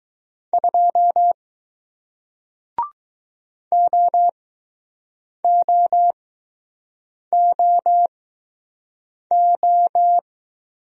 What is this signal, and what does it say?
Morse code, heard as a keyed tone, about 20 wpm, spaced as for 6 wpm, 700 Hz. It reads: 2 OOOO